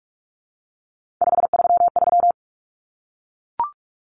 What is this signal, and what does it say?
Morse code, heard as a keyed tone, about 44 wpm, 700 Hz. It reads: 533
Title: Numbers - 3-Digit 45wpm